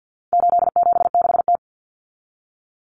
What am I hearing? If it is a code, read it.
Morse code, heard as a keyed tone, about 50 wpm, 700 Hz. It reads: 876T
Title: Common Cut Numbers - 4-Digit 50wpm